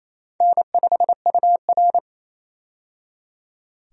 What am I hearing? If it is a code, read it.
Morse code, heard as a keyed tone, about 28 wpm, 700 Hz. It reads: N5UR
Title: USA Call Signs - 4 Characters - 28wpm